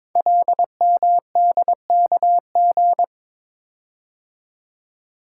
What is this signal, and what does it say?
Morse code, heard as a keyed tone, about 22 wpm, 700 Hz. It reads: LMDKG
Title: Code Groups - 5 Letters - 22wpm